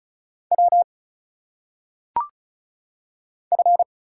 Morse code, heard as a keyed tone, about 35 wpm, 700 Hz. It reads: W F
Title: Single Letters - Phonetic Pronunciation 35wpm